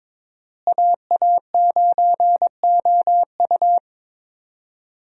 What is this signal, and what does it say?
Morse code, heard as a keyed tone, about 22 wpm, 700 Hz. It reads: AA9OU
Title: USA Call Signs - 5 Characters - Repeated 3x - 22wpm